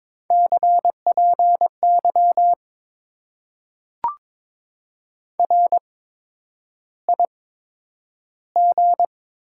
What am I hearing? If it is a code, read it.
Morse code, heard as a keyed tone, about 22 wpm, spaced as for 9 wpm, 700 Hz. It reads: CPY RIG
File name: Top 90 QSO Elements - 8x Character Spacing - 22wpm